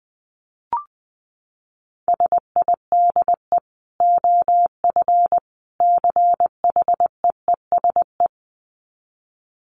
Morse code, heard as a keyed tone, about 20 wpm, 700 Hz. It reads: SIDE OF CHEESE
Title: Dit Challenge 20wpm